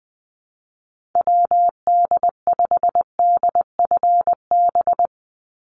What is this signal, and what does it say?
Morse code, heard as a keyed tone, about 20 wpm, 700 Hz. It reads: WD5DFB
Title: USA Call Signs 20wpm